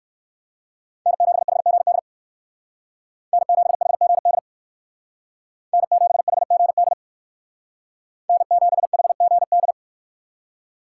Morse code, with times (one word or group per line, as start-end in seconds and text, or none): 1.1-2.0 N7HGD
3.3-4.4 N7HGD
5.7-6.9 N7HGD
8.3-9.7 N7HGD